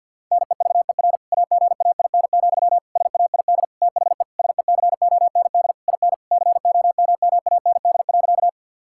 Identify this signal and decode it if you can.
Morse code, heard as a keyed tone, about 51 wpm, 700 Hz. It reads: NEVER AGAIN, SAID THE SECOND IN COMMAND.
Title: Sentences from Top 1600 Words 50wpm